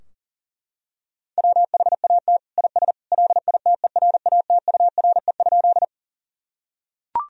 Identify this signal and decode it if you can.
Morse code, heard as a keyed tone, about 40 wpm, 700 Hz. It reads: WHAT IS LITERATURE?
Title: Sentences from Top 1000 Words - Review Part 3 - 40wpm